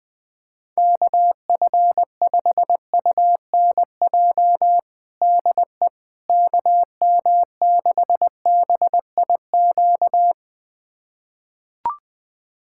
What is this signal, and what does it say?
Morse code, heard as a keyed tone, about 20 wpm, 700 Hz. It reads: KF5UNJ DE KM6BIQ
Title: USA Callsign DE Callsign 20wpm